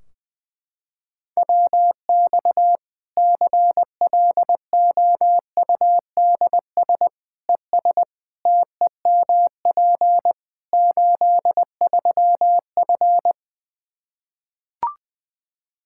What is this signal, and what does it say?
Morse code, heard as a keyed tone, about 20 wpm, 700 Hz. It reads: WX CLOUDS ES TEMP 83F
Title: K1USN SST QSO Practice 20wpm